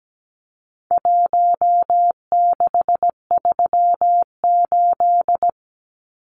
Morse code, heard as a keyed tone, about 17 wpm, 700 Hz. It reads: 1638
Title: Numbers - Contest Style - 4-Digit 17wpm